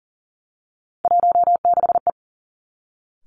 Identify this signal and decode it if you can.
Morse code, heard as a keyed tone, about 42 wpm, 700 Hz. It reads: 16E
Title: Three Challenging Letter-Number Combinations 40wpm